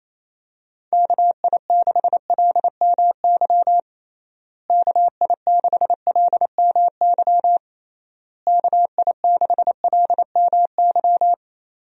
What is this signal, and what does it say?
Morse code, heard as a keyed tone, about 28 wpm, 700 Hz. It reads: KI6LMY KI6LMY KI6LMY
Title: USA Call Signs - 6 Characters - Repeated 3x - 28wpm